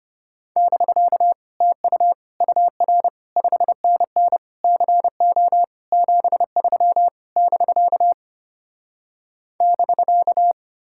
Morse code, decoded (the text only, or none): <BK> TU UR 5NN CO 73 <BK> <BK>